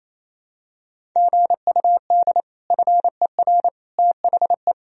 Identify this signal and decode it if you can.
Morse code, heard as a keyed tone, about 28 wpm, 700 Hz. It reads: GUD FER THE
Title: Sentences from Top 300 Words and Abbreviations 28wpm